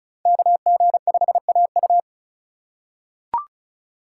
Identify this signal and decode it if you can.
Morse code, heard as a keyed tone, about 34 wpm, 700 Hz. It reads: KG5AU
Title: USA Call Signs - 5 Characters - 6x Character Spacing - 35wpm